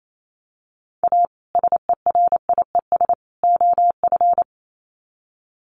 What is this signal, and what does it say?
Morse code, heard as a keyed tone, about 28 wpm, 700 Hz. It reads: A SERIES OF
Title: Sets of 3 Words 28wpm